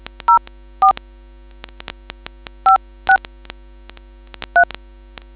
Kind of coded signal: DTMF